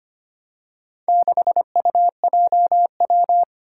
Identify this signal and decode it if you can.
Morse code, heard as a keyed tone, about 25 wpm, 700 Hz. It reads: 6UJW
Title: Four Challenging Letter-Number Combinations 25wpm